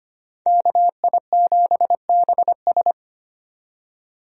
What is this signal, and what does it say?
Morse code, heard as a keyed tone, about 25 wpm, 700 Hz. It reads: KI7BS
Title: USA Call Signs - 5 Characters - 25wpm